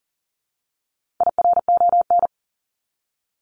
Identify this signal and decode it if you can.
Morse code, heard as a keyed tone, about 40 wpm, 700 Hz. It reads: IRON
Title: Top 1751-2000 Words 40wpm